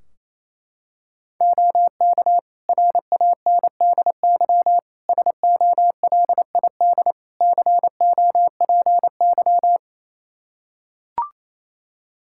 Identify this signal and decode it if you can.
Morse code, heard as a keyed tone, about 28 wpm, 700 Hz. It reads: OK RANDY SOLID COPY
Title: Practice QSO - Exchange 3 - More Info - 28wpm